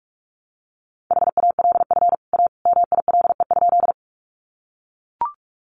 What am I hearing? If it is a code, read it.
Morse code, heard as a keyed tone, about 44 wpm, 700 Hz. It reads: HALF A MILE?
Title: Sentences from Top 400 Words 45wpm